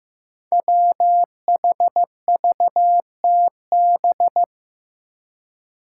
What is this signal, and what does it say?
Morse code, heard as a keyed tone, about 15 wpm, 700 Hz. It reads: WHVTB
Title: Five Challenging Letter-Number Combinations 15wpm